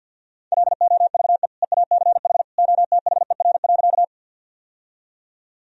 Morse code, heard as a keyed tone, about 51 wpm, 700 Hz. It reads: LOVE EACH OTHER.
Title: Sentences from Top 1000 Words - Review Part 1 - 50wpm